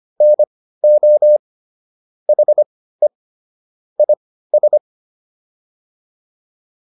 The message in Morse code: NO HE IS

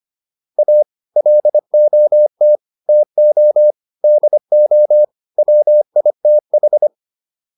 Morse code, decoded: A LOT TO DO WITH

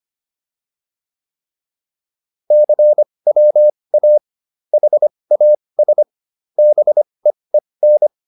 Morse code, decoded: CWA HAS BEEN

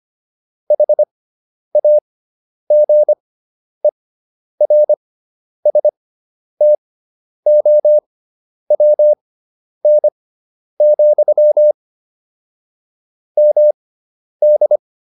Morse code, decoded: HAGERSTOWN, MD

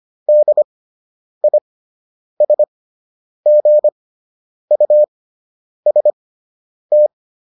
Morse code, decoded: DISGUST

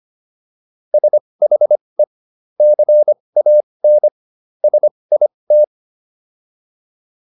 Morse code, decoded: SHE CAN SIT

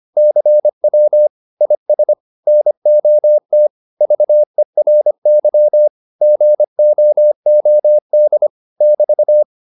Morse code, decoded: CW IS NOT VERY GOOD <BT>

